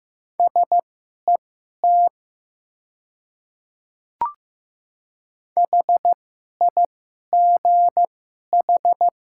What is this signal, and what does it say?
Morse code, heard as a keyed tone, about 15 wpm, spaced as for 11 wpm, 700 Hz. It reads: SET HIGH
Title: 2x Character Spacing - Top 200 Words - 15wpm